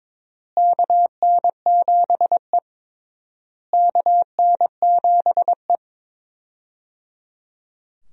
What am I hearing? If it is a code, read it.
Morse code, heard as a keyed tone, about 22 wpm, 700 Hz. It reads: KN7E KN7E
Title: USA Call Signs - 4 Characters - Repeated 3x - 22wpm